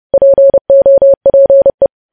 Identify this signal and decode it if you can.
Morse code, heard as a keyed tone, about 30 wpm, 550 Hz. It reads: POPE
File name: podcast / News headlines-1x-30x30